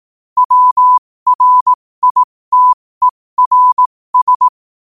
Morse code, heard as a keyed tone, about 18 wpm, spaced as for 15 wpm, 1000 Hz. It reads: WRITERS
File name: qotdfiles / 15wpm